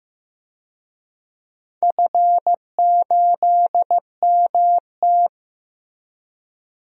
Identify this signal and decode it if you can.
Morse code, heard as a keyed tone, about 15 wpm, 700 Hz. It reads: F8MT
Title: DXCC-100 Call Signs - 4 Characters - Repeated 2x - 15wpm